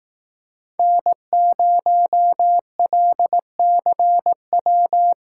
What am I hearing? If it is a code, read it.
Morse code, heard as a keyed tone, about 17 wpm, 700 Hz. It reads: N0LCW
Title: Speed Racing - USA Call Signs 15wpm